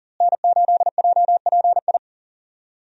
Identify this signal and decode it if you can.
Morse code, heard as a keyed tone, about 40 wpm, 700 Hz. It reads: N8JPI
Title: USA Call Signs - 5 Characters - Repeated 2x - 40wpm